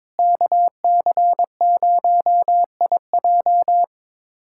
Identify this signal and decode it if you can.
Morse code, heard as a keyed tone, about 22 wpm, 700 Hz. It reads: KC0IJ
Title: USA Callsign DE Callsign - 5 Characters - 22wpm